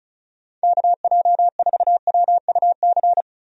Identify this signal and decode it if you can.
Morse code, heard as a keyed tone, about 36 wpm, 700 Hz. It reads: KJ4WUC
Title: USA Call Signs - 6 Characters - 35wpm